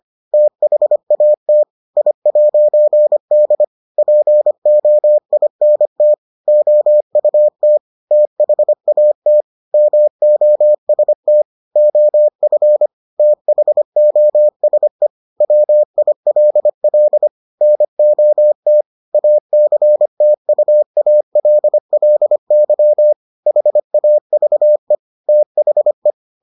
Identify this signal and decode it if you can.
Morse code, heard as a keyed tone, about 25 wpm, 600 Hz. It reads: THAT I'D POINT OUT THAT MOST OF THOSE WILL NOT ACTUALLY HAVE THE